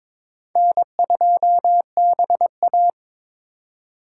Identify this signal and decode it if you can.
Morse code, heard as a keyed tone, about 22 wpm, 700 Hz. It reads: N2BA
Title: USA Call Signs - 3x Character Spacing - 22wpm